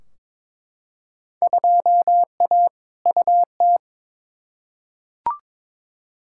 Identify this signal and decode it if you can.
Morse code, heard as a keyed tone, about 22 wpm, 700 Hz. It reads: 2A UT
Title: Field Day Exchanges - 22wpm